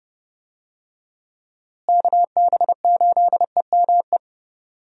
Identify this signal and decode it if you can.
Morse code, heard as a keyed tone, about 30 wpm, 700 Hz. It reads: KB8EME
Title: USA Call Signs - 6 Characters - Repeated 3x - 30wpm